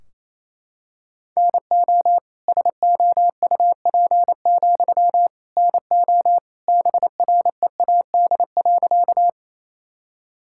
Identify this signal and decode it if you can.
Morse code, heard as a keyed tone, about 27 wpm, 700 Hz. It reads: NO SOUP, NO BREAD.
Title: Sentences from Top 1600 Words 28wpm